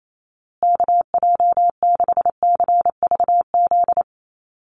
Morse code, heard as a keyed tone, about 28 wpm, 700 Hz. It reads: KJ6CVZ
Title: USA Call Signs Repeated 2x 28wpm